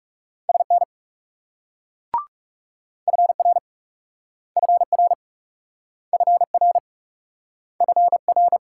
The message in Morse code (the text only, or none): SN FR FR FR FR